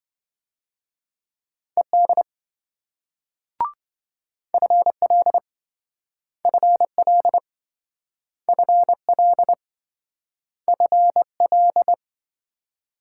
Morse code, heard as a keyed tone, about 27 wpm, 700 Hz. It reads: ED FL FL FL FL